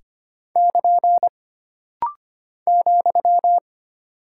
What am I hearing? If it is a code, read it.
Morse code, heard as a keyed tone, about 25 wpm, 700 Hz. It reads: <KN> ,
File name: Single Punctuation-Prosign - Rapid-Fire - 25wpm